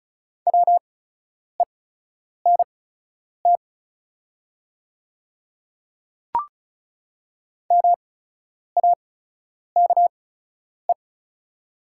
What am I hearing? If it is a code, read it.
Morse code, heard as a keyed tone, about 36 wpm, spaced as for 10 wpm, 690 Hz. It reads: WENT MAKE
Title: 8x Character Spacing - Top 200 Words - 35wpm